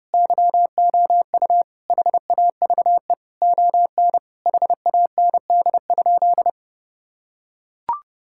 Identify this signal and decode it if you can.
Morse code, heard as a keyed tone, about 30 wpm, 700 Hz. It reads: YOU HAVE ON HAND?
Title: Sentences from Top 1000 Words - Review Part 5 - 30wpm